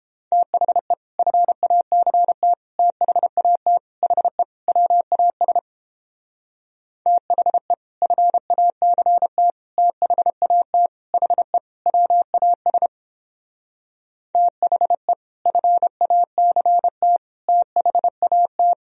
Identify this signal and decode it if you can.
Morse code, heard as a keyed tone, about 29 wpm, 700 Hz. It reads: THE FACT THAT HE WAS THE FACT THAT HE WAS THE FACT THAT